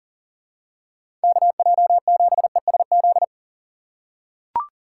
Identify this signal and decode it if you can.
Morse code, heard as a keyed tone, about 40 wpm, 700 Hz. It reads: KJ7ESZ
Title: USA Call Signs - 3x Character Spacing - 40wpm